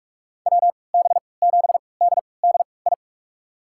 Morse code, heard as a keyed tone, about 46 wpm, spaced as for 26 wpm, 700 Hz. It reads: WB7DDI
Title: USA Call Signs - 3x Character Spacing - 45wpm